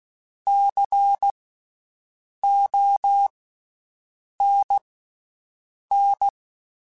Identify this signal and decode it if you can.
Morse code, heard as a keyed tone, about 16 wpm, spaced as for 6 wpm, 800 Hz. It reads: CONN
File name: Special INGs / Special INGs-mp3s / VerySlowToFaster Ing @15